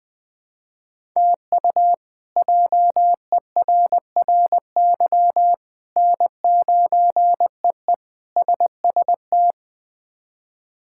Morse code, heard as a keyed tone, about 20 wpm, 700 Hz. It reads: TU JERRY N9EE SST
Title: K1USN SST QSO Practice - 20wpm